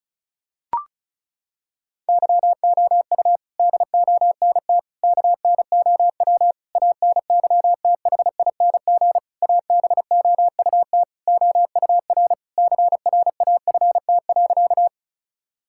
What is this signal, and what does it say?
Morse code, heard as a keyed tone, about 34 wpm, 700 Hz. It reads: YOU DONT KNOW ANYTHING ABOUT OUR CRAFT.
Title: Sentences from Top 1000 Words - Review Part 6 - 35wpm